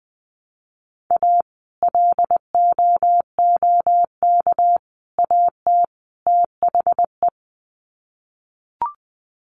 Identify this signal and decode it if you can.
Morse code, heard as a keyed tone, about 20 wpm, 700 Hz. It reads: A LOOK AT THE
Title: Sets of 4 Words 20wpm